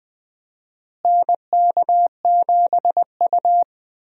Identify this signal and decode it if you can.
Morse code, heard as a keyed tone, about 20 wpm, 700 Hz. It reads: NK7U
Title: USA Call Signs - 4 Characters - Repeated 3x - 20wpm